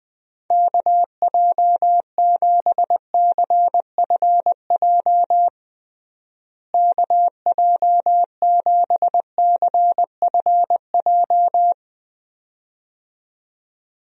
Morse code, heard as a keyed tone, about 20 wpm, 700 Hz. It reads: KJ7CFJ KJ7CFJ
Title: USA Call Signs - 6 Characters - Repeated 2x - 20wpm